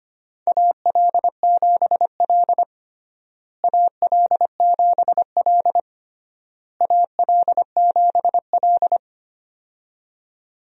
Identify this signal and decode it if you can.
Morse code, heard as a keyed tone, about 25 wpm, 700 Hz. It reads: AL7L AL7L AL7L
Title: USA Call Signs - 4 Characters - Repeated 3x - 25wpm